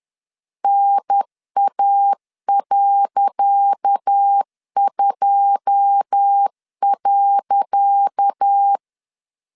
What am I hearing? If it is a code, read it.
Morse code, heard as a keyed tone, about 10 wpm, 800 Hz. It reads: NA.2.
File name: LEZIONI / 10 SIGNS / 10C DOT 10wpm 2940